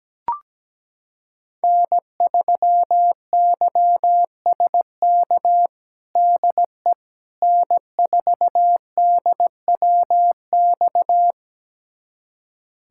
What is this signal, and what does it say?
Morse code, heard as a keyed tone, about 17 wpm, 700 Hz. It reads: N3YSK DE N4DWX
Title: USA Callsign DE Callsign - 5 Characters - 17wpm